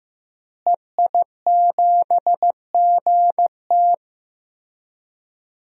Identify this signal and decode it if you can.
Morse code, heard as a keyed tone, about 15 wpm, 700 Hz. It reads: EI7GT
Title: DXCC-100 Call Signs - 5 Characters - Repeated 3x - 15wpm